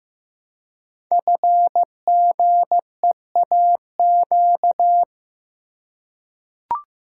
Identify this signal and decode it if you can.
Morse code, heard as a keyed tone, about 15 wpm, 700 Hz. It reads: FGEAQ
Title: Code Groups - 5 Letters - 15wpm